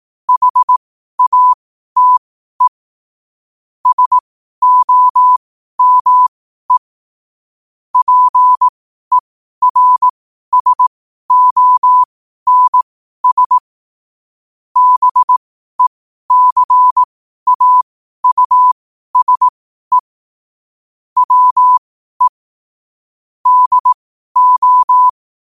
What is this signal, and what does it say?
Morse code, heard as a keyed tone, about 18 wpm, spaced as for 12 wpm, 1000 Hz. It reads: HATE SOME PERSONS BECAUSE WE DO